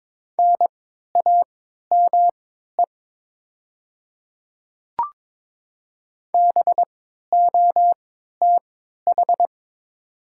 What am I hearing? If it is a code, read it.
Morse code, heard as a keyed tone, about 22 wpm, spaced as for 13 wpm, 700 Hz. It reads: NAME BOTH
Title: 3x Character Spacing - Top 300 Words - 22wpm